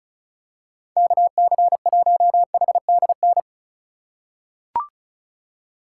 Morse code, heard as a keyed tone, about 34 wpm, 700 Hz. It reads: KC1HDN
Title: USA Call Signs - 6 Characters - 3x Character Spacing - 35wpm